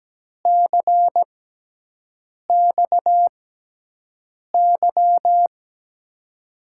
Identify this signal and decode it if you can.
Morse code, heard as a keyed tone, about 17 wpm, spaced as for 6 wpm, 700 Hz. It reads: CXY
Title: USA Call Signs - 6 Characters - 6x Character Spacing - 17wpm